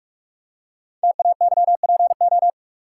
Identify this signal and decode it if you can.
Morse code, heard as a keyed tone, about 44 wpm, 690 Hz. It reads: TAYPO
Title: Code Groups - 5 Letters - 45wpm